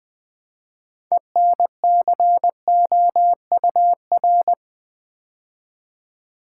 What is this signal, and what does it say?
Morse code, heard as a keyed tone, about 20 wpm, 700 Hz. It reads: ENCOUR
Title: Build A Word - 3 Letters At A Time - 9-Letter Words - 20wpm